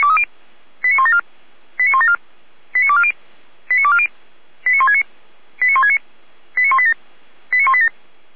Five-tone selective call (ZVEI), heard as quick runs of five tones, tones of about 70 ms. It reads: 89174 89174 89240 89240 89169 89169 89187 89187